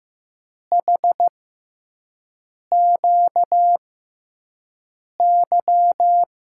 Morse code, heard as a keyed tone, about 15 wpm, spaced as for 5 wpm, 700 Hz. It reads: HQY
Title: USA Call Signs - 6x Character Spacing - 15wpm